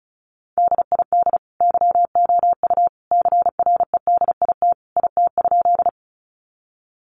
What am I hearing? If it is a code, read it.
Morse code, heard as a keyed tone, about 34 wpm, 700 Hz. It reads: DID YOU CREDIT IT?
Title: Sentences from Top 1000 Words 35wpm